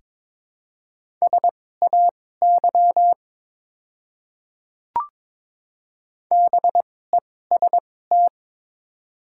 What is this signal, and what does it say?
Morse code, heard as a keyed tone, about 21 wpm, spaced as for 16 wpm, 690 Hz. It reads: SAY BEST
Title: 2x Character Spacing - Top 500 Words - 22wpm